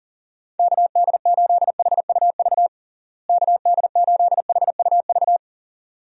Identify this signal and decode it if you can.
Morse code, heard as a keyed tone, about 37 wpm, 700 Hz. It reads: KD8HUV KD8HUV
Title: USA Call Signs - 6 Characters - Repeated 2x - 40wpm